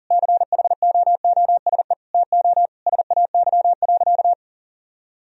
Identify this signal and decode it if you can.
Morse code, heard as a keyed tone, about 40 wpm, 700 Hz. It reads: CHOOSE TO SAY.
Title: Sentences from Top 1000 Words - Review Part 4 - 40wpm